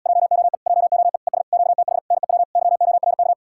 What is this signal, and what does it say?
Morse code, heard as a keyed tone, about 78 wpm, 700 Hz. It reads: JOE JOE ES QTH NEW YORK